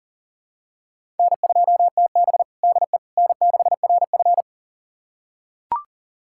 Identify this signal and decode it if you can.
Morse code, heard as a keyed tone, about 40 wpm, 700 Hz. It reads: N2TB DE N6RF
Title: USA Callsign DE Callsign - 4 Characters - 40wpm